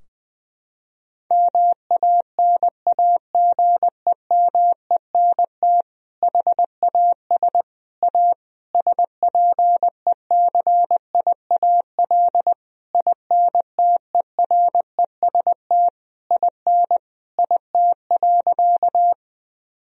Morse code, heard as a keyed tone, about 20 wpm, 700 Hz. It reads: MANAGEMENT HAS A SPECIAL INTEREST IN IT.
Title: Sentences from Top 1000 Words - Review Part 4 - 20wpm